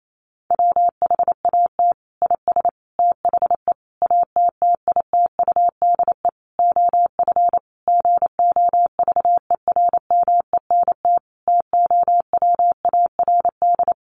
Morse code, heard as a keyed tone, about 28 wpm, 700 Hz. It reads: WHAT IS THE ATTITUDE OF GOVERMENT TOWARD
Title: Sentences from Top 1300 Words 28wpm